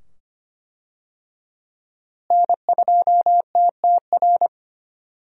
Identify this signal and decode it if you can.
Morse code, heard as a keyed tone, about 25 wpm, 700 Hz. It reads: N2TTR
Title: USA Call Signs - 5 Characters - Repeated 2x - 25wpm